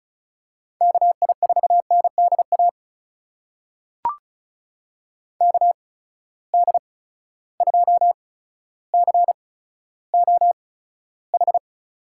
Morse code, heard as a keyed tone, about 36 wpm, spaced as for 15 wpm, 700 Hz. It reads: KI4NDA KD2COH
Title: USA Call Signs - 6 Characters - 8x Character Spacing - 35wpm